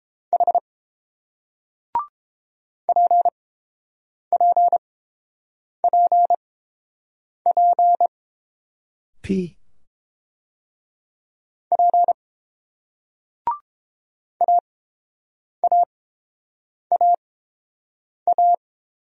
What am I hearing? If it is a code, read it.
Morse code, heard as a keyed tone, about 30 wpm, spaced as for 7 wpm, 700 Hz. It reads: H PPPP P AAAA